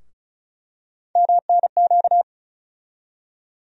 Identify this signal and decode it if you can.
Morse code, heard as a keyed tone, about 36 wpm, 700 Hz. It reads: MNQ
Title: USA Call Sign Postfixes - 3 Letters 35wpm